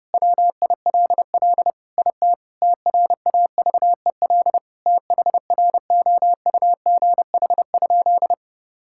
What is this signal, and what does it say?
Morse code, heard as a keyed tone, about 30 wpm, 700 Hz. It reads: WILL IT TRAVEL THROUGH?
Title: Sentences from Top 1000 Words - Review Part 6 - 30wpm